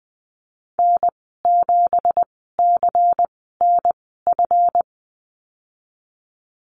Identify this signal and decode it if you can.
Morse code, heard as a keyed tone, about 20 wpm, spaced as for 15 wpm, 700 Hz. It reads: N7CNF
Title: 179 Lesson 29g - Slash - Call Signs 20wpm